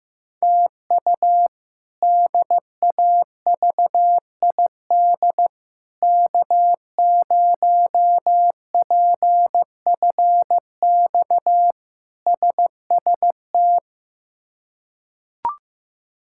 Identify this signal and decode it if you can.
Morse code, heard as a keyed tone, about 15 wpm, 700 Hz. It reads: TU DAVID K0PFX SST
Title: K1USN SST QSO Practice - 15wpm